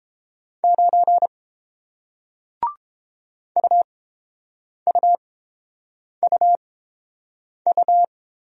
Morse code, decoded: 9 UUUU

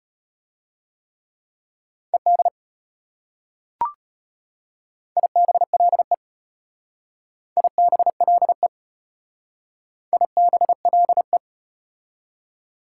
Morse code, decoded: ED IBLE IBLE IBLE

